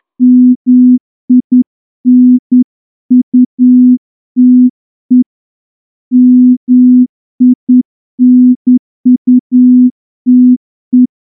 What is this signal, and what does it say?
Morse code, heard as a keyed tone, about 10 wpm, 250 Hz. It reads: MINUTE MINUTE
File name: morse-training-10WPM